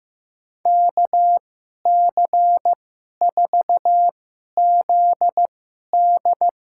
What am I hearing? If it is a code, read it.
Morse code, heard as a keyed tone, about 15 wpm, spaced as for 11 wpm, 700 Hz. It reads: KC4ZD
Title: USA Call Signs - 2x Character Spacing - 15wpm